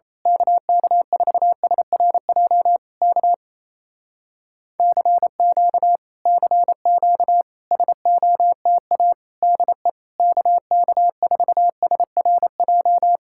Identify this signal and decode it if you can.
Morse code, heard as a keyed tone, about 27 wpm, 700 Hz. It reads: KK4SRJ K CQ CQ SOTA DE KK4SRJ
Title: Speed-Racing Common QSO Phrases Encore - 28wpm